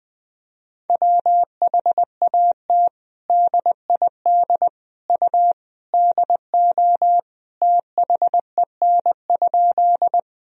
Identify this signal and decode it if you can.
Morse code, heard as a keyed tone, about 20 wpm, 700 Hz. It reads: WHAT DID U DO THEN?
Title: Sentences from Top 100 Words and Abbreviations 20wpm